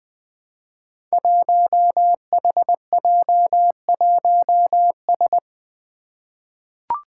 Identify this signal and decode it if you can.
Morse code, heard as a keyed tone, about 20 wpm, 700 Hz. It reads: 1HJ1S
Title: Five Challenging Letter-Number Combinations 20wpm